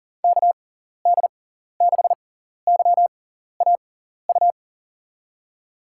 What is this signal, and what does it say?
Morse code, heard as a keyed tone, about 40 wpm, spaced as for 14 wpm, 700 Hz. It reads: KD6YAU
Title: USA Call Signs - 6 Characters - 6x Character Spacing - 40wpm